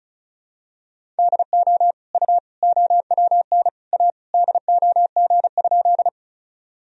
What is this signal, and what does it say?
Morse code, heard as a keyed tone, about 34 wpm, 700 Hz. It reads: DO U OWN A DOG?
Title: Sentences from Top 400 Words and Abbreviations 35wpm